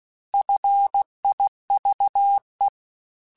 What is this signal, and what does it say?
Morse code, heard as a keyed tone, about 16 wpm, 800 Hz. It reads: FIVE
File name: F Runs / F Runs mp3s / F Runs @15